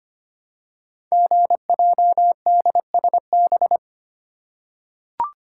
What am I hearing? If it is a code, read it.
Morse code, heard as a keyed tone, about 25 wpm, 700 Hz. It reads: GJDSB
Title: Code Groups - 5 Letters - 25wpm